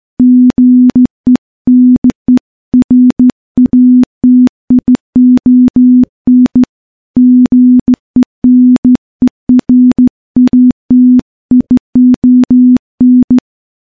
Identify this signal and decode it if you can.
Morse code, heard as a keyed tone, about 14 wpm, 250 Hz. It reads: GENERATION GENERATION
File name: morse / morse-training-15WPM